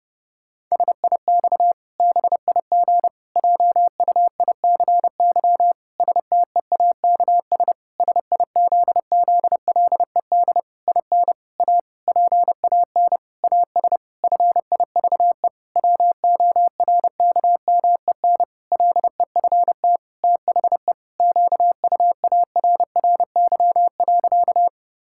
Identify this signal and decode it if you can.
Morse code, heard as a keyed tone, about 30 wpm, 700 Hz. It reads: SIX BIG JUICY STEAKS SIZZLED IN A PAN AS FIVE WORKMEN LEFT THE QUARRY.
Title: Pangram Challenge v2 - 30wpm